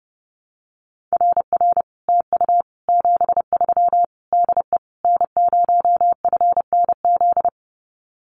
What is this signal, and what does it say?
Morse code, heard as a keyed tone, about 32 wpm, 700 Hz. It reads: RR TU 73 DE N0FNZ
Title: Common QSO Phrases Encore 30wpm